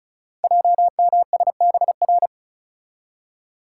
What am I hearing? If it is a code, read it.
Morse code, heard as a keyed tone, about 36 wpm, 700 Hz. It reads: JMSBR
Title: Code Groups - 5 Letters - 35wpm